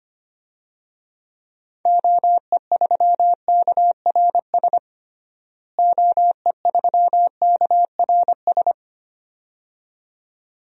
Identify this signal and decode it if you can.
Morse code, heard as a keyed tone, about 25 wpm, 700 Hz. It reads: OE3KRS OE3KRS
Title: DXCC-100 Call Signs - 6 Characters - Repeated 2x - 25wpm